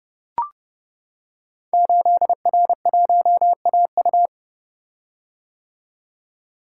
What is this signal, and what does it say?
Morse code, heard as a keyed tone, about 30 wpm, 700 Hz. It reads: 8R1AU
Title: Code Groups - 5 Letter:Numbers - 30wpm